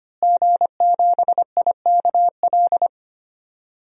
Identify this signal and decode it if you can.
Morse code, heard as a keyed tone, about 25 wpm, 700 Hz. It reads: G7IKL
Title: DXCC-100 Call Signs - 5 Characters - Repeated 2x - 25wpm